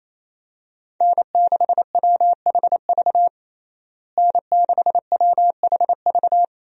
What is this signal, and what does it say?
Morse code, heard as a keyed tone, about 28 wpm, 700 Hz. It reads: N6WHV N6WHV
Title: USA Call Signs - 5 Characters - Repeated 2x - 28wpm